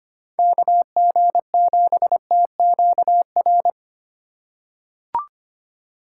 Morse code, heard as a keyed tone, about 25 wpm, 700 Hz. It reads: KG7TQR
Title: USA Call Signs - 6 Characters - 2x Character Spacing - 25wpm